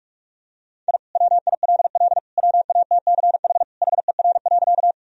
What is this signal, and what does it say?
Morse code, heard as a keyed tone, about 42 wpm, 700 Hz. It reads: I WILL WATCH HER.
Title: Sentences from Top 1000 Words - Review Part 1 - 45wpm